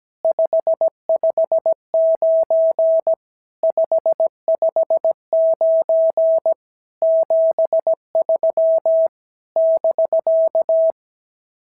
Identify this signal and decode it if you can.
Morse code, heard as a keyed tone, about 17 wpm, 650 Hz. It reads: 559 559 73 <BK>
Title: SOTA QSO Practice 17wpm